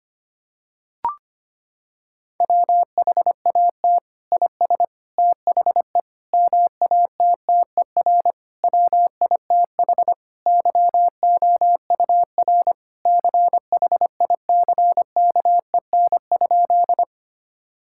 Morse code, heard as a keyed tone, about 25 wpm, 700 Hz. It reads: WHAT IS THE MATTER WITH YOUR CHICKEN?
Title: Sentences from Top 1000 Words - Review Part 5 - 25wpm